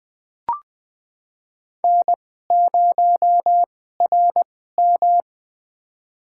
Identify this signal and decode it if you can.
Morse code, heard as a keyed tone, about 20 wpm, spaced as for 14 wpm, 700 Hz. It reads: N0RM
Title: USA Call Signs - 4 Characters - 2x Character Spacing - 20wpm